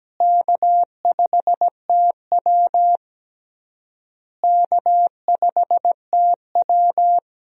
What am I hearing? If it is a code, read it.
Morse code, heard as a keyed tone, about 17 wpm, 700 Hz. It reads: K5TW K5TW
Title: USA Call Signs - 4 Characters - Repeated 3x - 17wpm